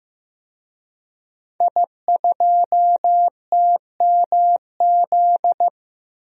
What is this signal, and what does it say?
Morse code, heard as a keyed tone, about 16 wpm, 700 Hz. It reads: I2TMZ
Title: DXCC-100 Call Signs - 5 Characters - Repeated 3x - 15wpm